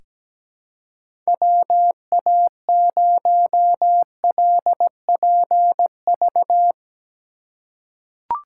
Morse code, heard as a keyed tone, about 17 wpm, 700 Hz. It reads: WA0LPV
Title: K1USN SST QSO Practice - 17wpm